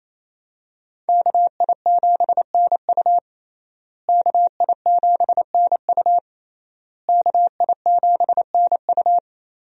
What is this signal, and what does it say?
Morse code, heard as a keyed tone, about 28 wpm, 700 Hz. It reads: KI7NU KI7NU KI7NU
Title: USA Call Signs - 5 Characters - Repeated 3x - 28wpm